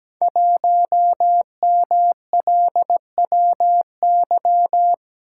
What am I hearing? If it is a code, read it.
Morse code, heard as a keyed tone, about 17 wpm, 700 Hz. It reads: 1MLWY
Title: Code Groups - 5 Letter:Numbers - 17wpm